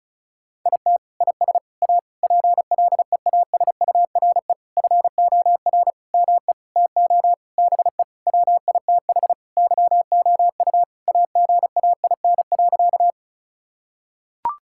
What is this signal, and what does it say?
Morse code, heard as a keyed tone, about 36 wpm, 700 Hz. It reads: IT IS A PLEASURE FOR ME TO BE WITH YOU AGAIN.
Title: Sentences from Top 1600 Words 35wpm